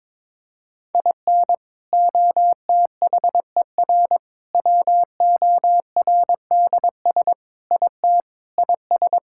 Morse code, heard as a keyed tone, about 22 wpm, 700 Hz. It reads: IN OTHER WORDS IT IS
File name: practice / 22wpm / Sentences with 3-Word Phrases - 22wpm